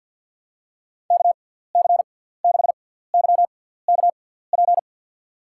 Morse code, heard as a keyed tone, about 50 wpm, spaced as for 17 wpm, 700 Hz. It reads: KC6YKP
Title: USA Call Signs - 6x Character Spacing - 50wpm